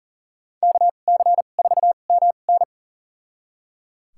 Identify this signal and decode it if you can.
Morse code, heard as a keyed tone, about 37 wpm, spaced as for 28 wpm, 700 Hz. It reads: KC4MN
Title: USA Call Signs - 5 Characters - 2x Character Spacing - 40wpm